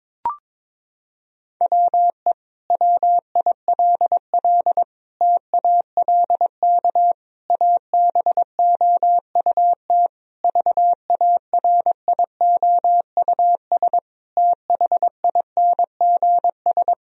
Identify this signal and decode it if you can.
Morse code, heard as a keyed tone, about 22 wpm, 700 Hz. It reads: WE WILL TALK ABOUT VARIOUS THINGS
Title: Sentences from Top 1000 Words - Review Part 4 - 22wpm